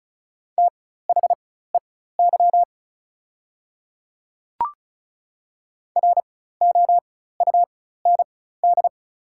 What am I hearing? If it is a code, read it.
Morse code, heard as a keyed tone, about 36 wpm, spaced as for 16 wpm, 700 Hz. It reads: THEY ROUND
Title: 4x Character Spacing - Top 200 Words - 35wpm